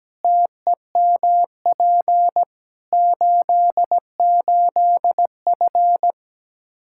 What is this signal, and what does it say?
Morse code, heard as a keyed tone, about 17 wpm, 700 Hz. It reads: TEMP 88F
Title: K1USN SST QSO Practice 17wpm